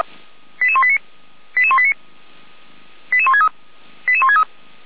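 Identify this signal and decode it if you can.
Five-tone selective call (ZVEI), heard as quick runs of five tones, tones of about 70 ms. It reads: 88189 88189 88163 88163
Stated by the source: ZVEI five-tone selective call 88191, received by SDR.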